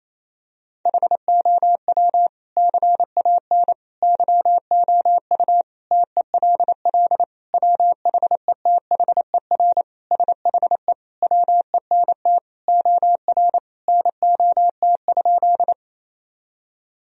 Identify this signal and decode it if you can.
Morse code, heard as a keyed tone, about 28 wpm, 700 Hz. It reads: HOW CAN YOU TELL WHETHER SHE WENT OR NOT?
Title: Sentences from Top 1000 Words - Review Part 3 - 28wpm